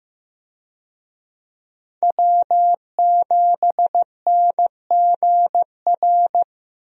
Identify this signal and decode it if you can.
Morse code, heard as a keyed tone, about 15 wpm, 700 Hz. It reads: W7NGR
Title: USA Call Signs - 5 Characters - Repeated 2x - 15wpm